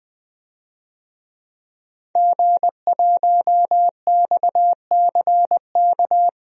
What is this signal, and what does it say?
Morse code, heard as a keyed tone, about 21 wpm, 700 Hz. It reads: G1XCK
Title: DXCC-100 Call Signs - 5 Characters - 20wpm